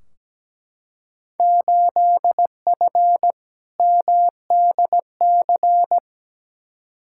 Morse code, heard as a keyed tone, about 17 wpm, 700 Hz. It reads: 8F MDC
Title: Field Day Exchanges Encore - 17wpm